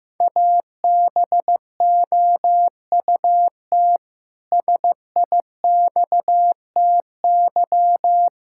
Morse code, heard as a keyed tone, about 15 wpm, 700 Hz. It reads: ABOUT SIXTY
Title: Pangram Challenge v2 - 15wpm